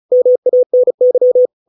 Morse code, heard as a keyed tone, about 34 wpm, 500 Hz. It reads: MANY